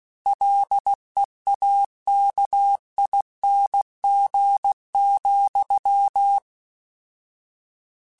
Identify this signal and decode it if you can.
Morse code, heard as a keyed tone, about 16 wpm, 800 Hz. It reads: LEAKING,
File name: Combinations / Combinations-mp3s / Combinations@15